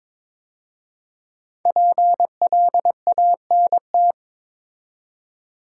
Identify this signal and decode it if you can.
Morse code, heard as a keyed tone, about 22 wpm, 700 Hz. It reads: PLANT
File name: 5 Letter Words - First 3 Letters Given - 22wpm